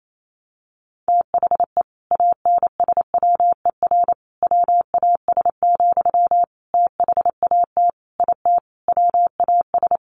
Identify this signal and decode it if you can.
Morse code, heard as a keyed tone, about 28 wpm, 700 Hz. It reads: THE ANSWER WAS, THAT IT WAS
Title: Sentences from Top 1100 Words 28wpm